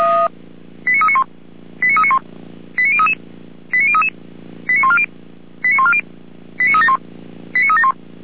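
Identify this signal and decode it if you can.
Five-tone selective call (ZVEI), heard as quick runs of five tones, tones of about 70 ms. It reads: 89391 89391 89033 89033 89140 89140 89371 89371
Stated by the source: ZVEI five-tone selective call received on an SDR.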